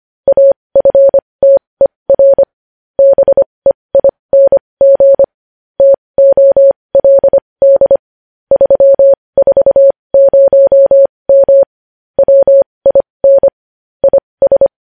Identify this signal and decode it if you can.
Morse code, heard as a keyed tone, about 25 wpm, spaced as for 20 wpm, 550 Hz. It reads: AFTER BEING TOLD 340M WIN IS
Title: News Headlines at 20